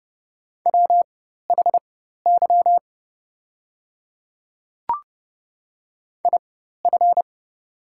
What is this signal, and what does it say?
Morse code, heard as a keyed tone, about 30 wpm, spaced as for 14 wpm, 700 Hz. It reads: WHY IF
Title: 4x Character Spacing - Top 200 Words - 30wpm